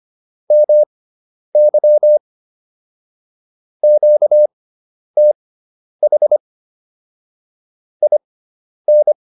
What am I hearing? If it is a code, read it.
Morse code, heard as a keyed tone, about 25 wpm, spaced as for 10 wpm, 600 Hz. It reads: MY QTH IN